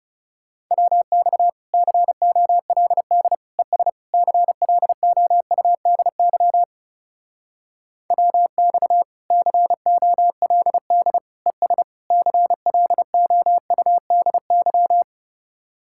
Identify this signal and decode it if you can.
Morse code, heard as a keyed tone, about 38 wpm, 700 Hz. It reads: WX COLD ES CLOUDY WX COLD ES CLOUDY